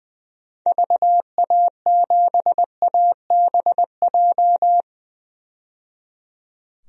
Morse code, heard as a keyed tone, about 21 wpm, 700 Hz. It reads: VA7ABJ
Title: DXCC-100 Call Signs - 6 Characters - 20wpm